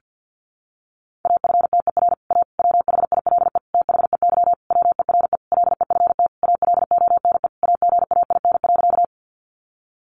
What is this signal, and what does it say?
Morse code, heard as a keyed tone, about 50 wpm, 700 Hz. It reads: AFTER A WHILE THEY WERE LEFT ALONE AGAIN.
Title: Sentences from Top 1000 Words 50wpm